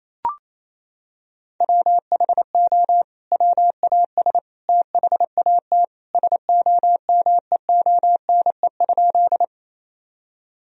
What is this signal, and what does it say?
Morse code, heard as a keyed tone, about 28 wpm, 700 Hz. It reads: WHO WAS THAT SOMEONE?
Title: Sentences from Top 1000 Words - Review Part 4 - 28wpm